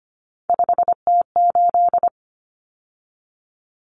Morse code, heard as a keyed tone, about 25 wpm, 700 Hz. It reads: 5T8
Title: Common Cut Numbers - 3-Digit 25wpm